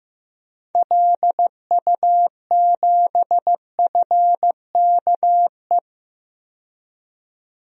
Morse code, heard as a keyed tone, about 15 wpm, 690 Hz. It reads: LU7FKE
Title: DXCC-100 Call Signs - 6 Characters - 15wpm